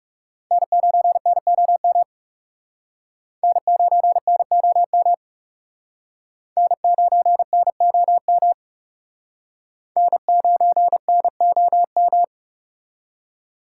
Morse code, heard as a keyed tone, about 37 wpm, 700 Hz. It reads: N9NOM N9NOM N9NOM N9NOM
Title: Speed Racing - USA 1x3 Call Signs 30wpm